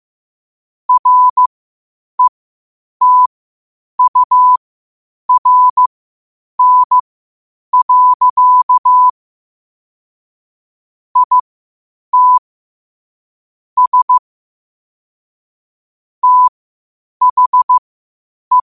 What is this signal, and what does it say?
Morse code, heard as a keyed tone, about 15 wpm, spaced as for 8 wpm, 1000 Hz. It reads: RETURN. IT S THE
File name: qotdfiles / 8wpm